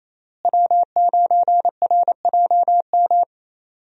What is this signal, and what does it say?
Morse code, heard as a keyed tone, about 28 wpm, 700 Hz. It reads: W9RJM
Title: USA Call Signs - 5 Characters - Repeated 2x - 28wpm